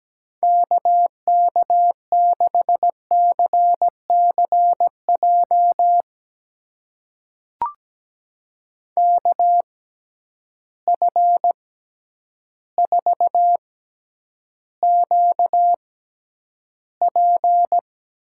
Morse code, heard as a keyed tone, about 17 wpm, spaced as for 9 wpm, 700 Hz. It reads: KK6CCJ KF4QP
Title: USA Call Signs - 6 Characters - 6x Character Spacing - 17wpm